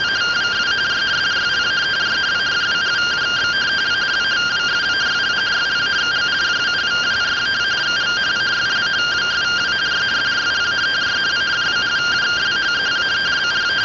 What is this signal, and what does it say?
Two-tone teleprinter signal, about 170 Hz shift.